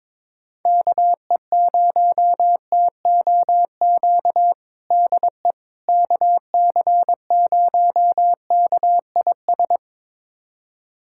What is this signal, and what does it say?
Morse code, heard as a keyed tone, about 22 wpm, 700 Hz. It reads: KE0TOQ DE KC0KIS
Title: USA Callsign DE Callsign - 6 Characters - 22wpm